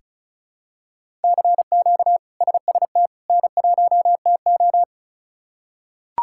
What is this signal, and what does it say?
Morse code, heard as a keyed tone, about 36 wpm, 700 Hz. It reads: CQ SST N1TO
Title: K1USN SST QSO Practice - 35wpm